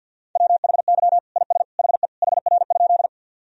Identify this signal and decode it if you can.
Morse code, heard as a keyed tone, about 52 wpm, 700 Hz. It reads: WHY IS HE HR?
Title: Sentences from Top 200 Words and Abbreviations 50wpm